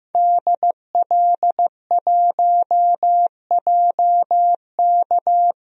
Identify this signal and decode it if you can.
Morse code, heard as a keyed tone, about 16 wpm, 700 Hz. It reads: DL1JK
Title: DXCC-100 Call Signs - 5 Characters - 15wpm